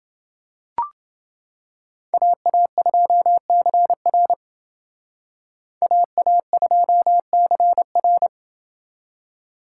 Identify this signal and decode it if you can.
Morse code, heard as a keyed tone, about 28 wpm, 700 Hz. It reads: AA2CR AA2CR